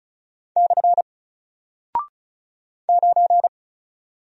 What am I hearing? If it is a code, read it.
Morse code, heard as a keyed tone, about 34 wpm, 700 Hz. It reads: / 9
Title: Single Character (Letter:Number:Punctuation:Prosign) Alt 35wpm